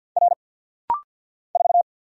Morse code, heard as a keyed tone, about 50 wpm, 690 Hz. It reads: R 4
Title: Single Letter-Number Alt - Rapid-Fire 50wpm